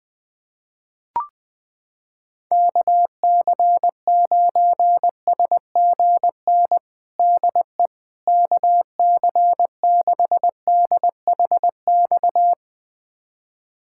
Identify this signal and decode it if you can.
Morse code, heard as a keyed tone, about 20 wpm, 700 Hz. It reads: KC9SGN DE KC6DHX
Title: USA Callsign DE Callsign 20wpm